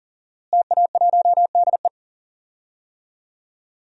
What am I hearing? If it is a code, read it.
Morse code, heard as a keyed tone, about 40 wpm, 700 Hz. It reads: TA1DE
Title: DXCC-100 Call Signs - 5 Characters - Repeated 2x - 40wpm